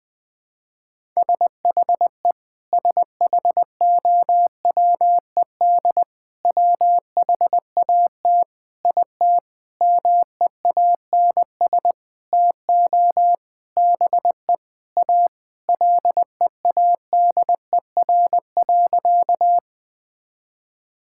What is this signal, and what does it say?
Morse code, heard as a keyed tone, about 21 wpm, 700 Hz. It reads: SHE SHOWED WHAT IT MEANS TO BE A LEADER.
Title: Sentences with 6-Word Phrases - Spoken Beginning - 20wpm